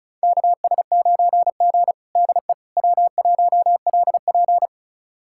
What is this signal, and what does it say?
Morse code, heard as a keyed tone, about 36 wpm, 700 Hz. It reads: KS9G DE W1LP
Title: USA Callsign DE Callsign - 4 Characters - 35wpm